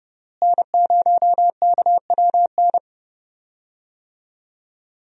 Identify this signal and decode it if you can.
Morse code, heard as a keyed tone, about 30 wpm, 700 Hz. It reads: N0KWN
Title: USA Call Signs - 5 Characters - Repeated 2x - 30wpm